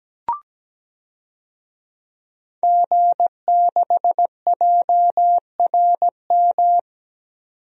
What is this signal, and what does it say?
Morse code, heard as a keyed tone, about 17 wpm, 700 Hz. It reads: G6JRM
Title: DXCC-100 Call Signs - 5 Characters - 17wpm